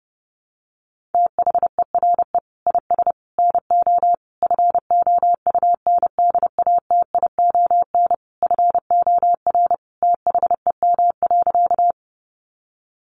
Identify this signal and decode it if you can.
Morse code, heard as a keyed tone, about 30 wpm, 700 Hz. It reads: THERE IS NO FOUNDATION FOR THEM.
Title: Sentences from Top 1200 Words 30wpm